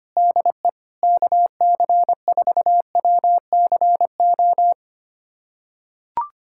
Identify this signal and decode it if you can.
Morse code, heard as a keyed tone, about 25 wpm, 700 Hz. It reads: DE KC4WCO
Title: USA Callsign DE Callsign - 6 Characters - 25wpm